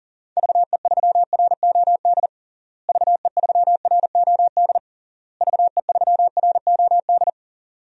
Morse code, heard as a keyed tone, about 40 wpm, 700 Hz. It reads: VE3ROD VE3ROD VE3ROD
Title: DXCC-100 Call Signs - 6 Characters - Repeated 3x - 40wpm